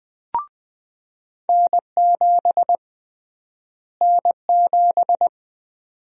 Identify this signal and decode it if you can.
Morse code, heard as a keyed tone, about 20 wpm, 700 Hz. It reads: N7 N7
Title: USA Call Sign Prefixes - Repeated 2x - 20wpm